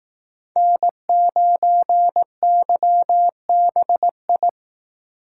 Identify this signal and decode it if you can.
Morse code, heard as a keyed tone, about 19 wpm, 700 Hz. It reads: N9YBI
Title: Speed Racing - USA 1x3 Call Signs 15wpm